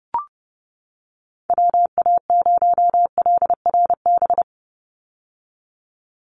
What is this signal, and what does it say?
Morse code, heard as a keyed tone, about 30 wpm, 700 Hz. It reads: WA0LRB
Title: USA Call Signs 30wpm